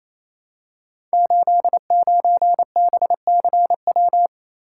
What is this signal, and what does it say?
Morse code, heard as a keyed tone, about 27 wpm, 700 Hz. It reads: 89BCW
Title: Code Groups - 5 Letter:Numbers - 28wpm